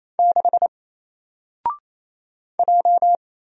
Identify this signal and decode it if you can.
Morse code, heard as a keyed tone, about 28 wpm, 700 Hz. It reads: 6 J
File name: Single-Letter-Number-28wpm